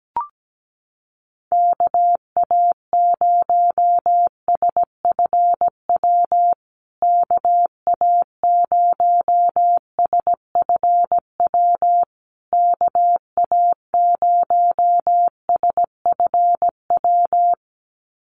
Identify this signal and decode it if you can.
Morse code, heard as a keyed tone, about 17 wpm, 700 Hz. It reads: KA0SFW KA0SFW KA0SFW
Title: USA Call Signs Repeated 3x 17wpm